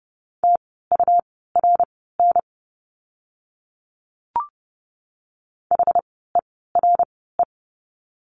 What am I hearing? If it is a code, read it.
Morse code, heard as a keyed tone, about 30 wpm, spaced as for 17 wpm, 700 Hz. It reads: TURN HERE
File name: course / 30:10wpm / 230 Lesson 37f - Letter x - Words 30wpm